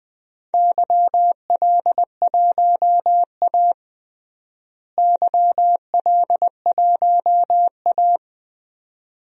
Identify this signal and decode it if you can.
Morse code, heard as a keyed tone, about 20 wpm, 700 Hz. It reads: YL1A YL1A
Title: DXCC-100 Call Signs - 4 Characters - Repeated 2x - 20wpm